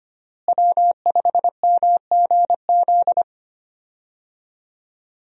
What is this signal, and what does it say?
Morse code, heard as a keyed tone, about 25 wpm, 700 Hz. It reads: W5MGZ
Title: USA Call Signs - 5 Characters - Repeated 2x - 25wpm